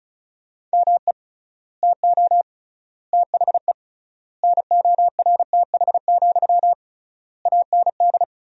Morse code, heard as a keyed tone, about 34 wpm, 700 Hz. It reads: ME TO THE NORTH, AND
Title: 3x Word-Spacing - Sentences from Top 400 Words - 35wpm